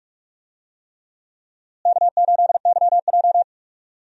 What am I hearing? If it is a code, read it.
Morse code, heard as a keyed tone, about 42 wpm, 700 Hz. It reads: K8YJ
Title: USA Call Signs - 4 Characters - Repeated 3x - 45wpm